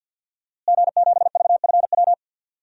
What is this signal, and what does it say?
Morse code, heard as a keyed tone, about 50 wpm, 700 Hz. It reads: G7VFW
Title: DXCC-100 Call Signs - 5 Characters - Repeated 3x - 50wpm